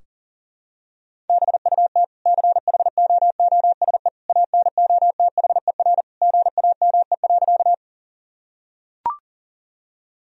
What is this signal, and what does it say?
Morse code, heard as a keyed tone, about 40 wpm, 700 Hz. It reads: BUT CHOOSE ANOTHER GAME.
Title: Sentences from Top 1000 Words - Review Part 4 - 40wpm